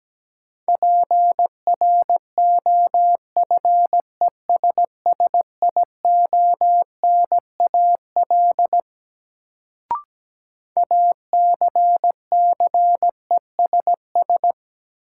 Morse code, heard as a keyed tone, about 17 wpm, 700 Hz. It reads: PROFESSIONAL ACCESS
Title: Sentences from Top 1000 Words - Review Part 4 - New Word Drill - 17wpm